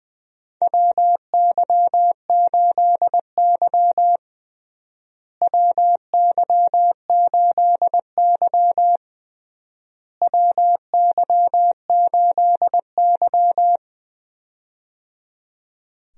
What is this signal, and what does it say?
Morse code, heard as a keyed tone, about 20 wpm, 700 Hz. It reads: WY8Y WY8Y WY8Y
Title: USA Call Signs - 4 Characters - Repeated 3x - 20wpm